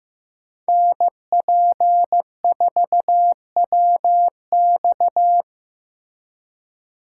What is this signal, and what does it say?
Morse code, heard as a keyed tone, about 18 wpm, 700 Hz. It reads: NP4WX
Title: Speed Racing - USA 2x2 Call Signs 15wpm